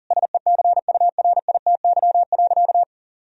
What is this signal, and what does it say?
Morse code, heard as a keyed tone, about 37 wpm, 700 Hz. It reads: SECURITY.
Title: Sentences from Top 1000 Words - Review Part 4 - 40wpm